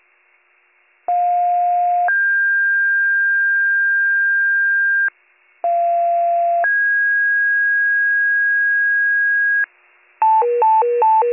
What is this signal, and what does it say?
Two-tone page, heard as a one-second tone, then a three-second tone, about 710 then 1600 Hz, about 690 then 1650 Hz. It is heard from 1.1 to 9.6 s.